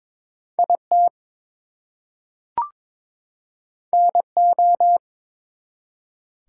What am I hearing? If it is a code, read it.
Morse code, heard as a keyed tone, about 22 wpm, 700 Hz. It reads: IT NO
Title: Two Letter Words - 22wpm